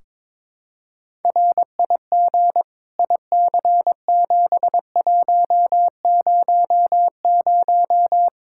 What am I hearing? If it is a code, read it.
Morse code, heard as a keyed tone, about 22 wpm, 700 Hz. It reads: RIG IC7100
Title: Practice QSO - Exchange 2 - Info - 22wpm